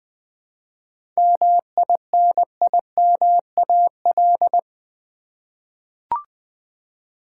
Morse code, heard as a keyed tone, about 20 wpm, 700 Hz. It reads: MINIMAL
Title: 7 Letter Words - First 3 Letters Given - 20wpm